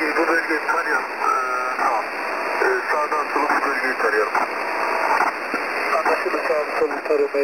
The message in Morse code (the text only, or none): none